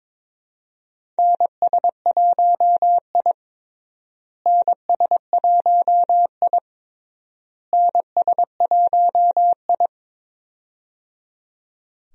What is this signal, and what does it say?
Morse code, heard as a keyed tone, about 21 wpm, 700 Hz. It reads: NS1I NS1I NS1I
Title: USA Call Signs - 4 Characters - Repeated 3x - 22wpm